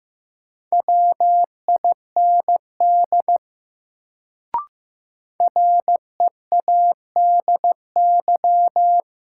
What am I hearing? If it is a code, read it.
Morse code, heard as a keyed tone, about 15 wpm, 700 Hz. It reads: WIND READY
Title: Sentences from Top 400 Words - New Word Drill - 15wpm